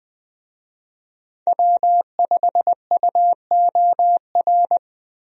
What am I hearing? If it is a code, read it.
Morse code, heard as a keyed tone, about 20 wpm, 700 Hz. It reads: W5UOR
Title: USA Call Signs - 5 Characters - Repeated 3x - 20wpm